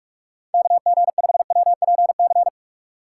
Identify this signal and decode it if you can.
Morse code, heard as a keyed tone, about 46 wpm, 700 Hz. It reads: KG5WPC
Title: USA Call Signs - 6 Characters - 2x Character Spacing - 45wpm